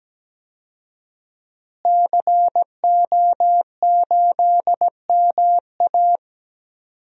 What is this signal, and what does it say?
Morse code, heard as a keyed tone, about 17 wpm, 700 Hz. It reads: CO8MA
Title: DXCC-100 Call Signs - 5 Characters - Repeated 2x - 17wpm